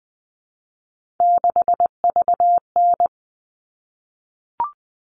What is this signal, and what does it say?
Morse code, heard as a keyed tone, about 20 wpm, 700 Hz. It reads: 6VN
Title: All Cut Numbers - 3-Digit 20wpm